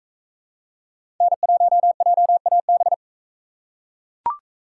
Morse code, heard as a keyed tone, about 42 wpm, 700 Hz. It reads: N1JAB